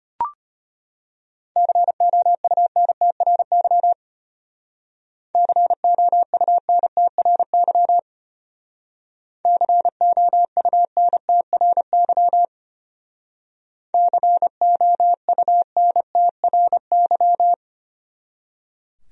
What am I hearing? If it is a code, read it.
Morse code, heard as a keyed tone, about 31 wpm, 700 Hz. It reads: COUNTRY COUNTRY COUNTRY COUNTRY
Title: Speed-Racing Top 500 Words 25wpm